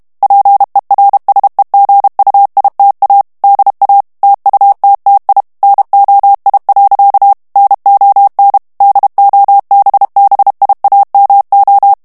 Morse code, heard as a keyed tone, about 32 wpm, 800 Hz. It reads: PERSEGUITA DA TUTTI NOI. NON DOBBIAMO
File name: ITALIEN texte clair 1 vitesse 30 mots minute